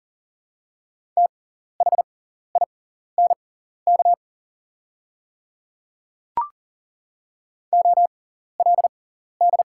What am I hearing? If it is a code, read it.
Morse code, heard as a keyed tone, about 40 wpm, spaced as for 14 wpm, 700 Hz. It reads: THINK OLD
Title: 6x Character Spacing - Top 200 Words - 40wpm